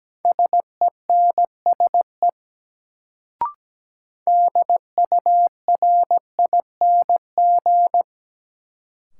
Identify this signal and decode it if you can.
Morse code, heard as a keyed tone, about 17 wpm, 700 Hz. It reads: SENSE DURING
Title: Sentences from Top 700 Words - New Word Drill - 17wpm